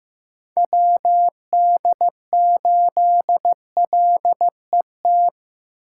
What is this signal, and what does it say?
Morse code, heard as a keyed tone, about 15 wpm, 700 Hz. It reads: WD8LET
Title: USA Callsign DE Callsign 15wpm